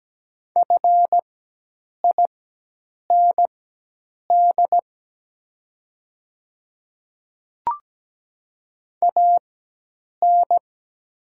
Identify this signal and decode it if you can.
Morse code, heard as a keyed tone, about 17 wpm, spaced as for 8 wpm, 700 Hz. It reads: FIND AN
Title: 4x Character Spacing - Top 100 Words - 17wpm